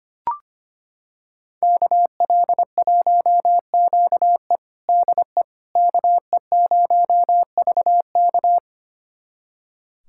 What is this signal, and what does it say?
Morse code, heard as a keyed tone, about 25 wpm, 700 Hz. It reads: KL1QE DE KE0VK
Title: USA Callsign DE Callsign - 5 Characters - 25wpm